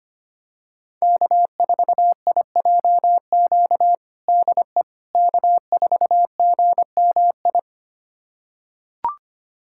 Morse code, heard as a keyed tone, about 25 wpm, 700 Hz. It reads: K4IJQ DE K4GMI
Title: USA Callsign DE Callsign 25wpm